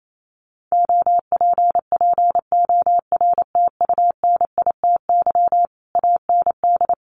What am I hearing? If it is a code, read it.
Morse code, heard as a keyed tone, about 28 wpm, 700 Hz. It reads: OPPORTUNITY AND
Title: Sentences from Top 1200 Words 28wpm